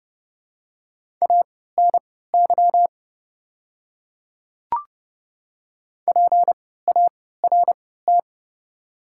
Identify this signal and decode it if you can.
Morse code, heard as a keyed tone, about 30 wpm, spaced as for 17 wpm, 700 Hz. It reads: ANY PART
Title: 3x Character Spacing - Top 200 Words - 30wpm